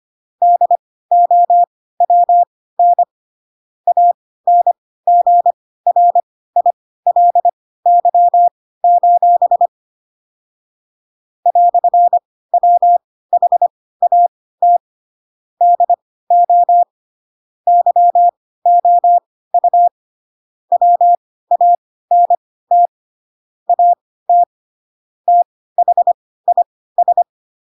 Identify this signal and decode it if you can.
Morse code, heard as a keyed tone, about 25 wpm, spaced as for 16 wpm, 700 Hz. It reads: DOWN ANGRILY: "WHAT DO YOU WANT AT THIS
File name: Pinocchio / Pinocchio ch6 10 25 ik1hge